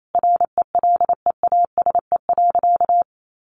Morse code, heard as a keyed tone, about 28 wpm, 700 Hz. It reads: RELEASE.
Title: Sentences from Top 1100 Words 28wpm